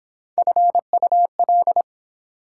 Morse code, heard as a keyed tone, about 24 wpm, 700 Hz. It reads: FUL